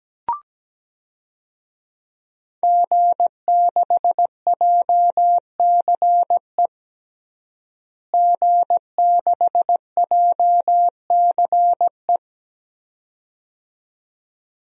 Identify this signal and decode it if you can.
Morse code, heard as a keyed tone, about 17 wpm, 700 Hz. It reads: G6JCE G6JCE
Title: DXCC-100 Call Signs - 5 Characters - Repeated 2x - 17wpm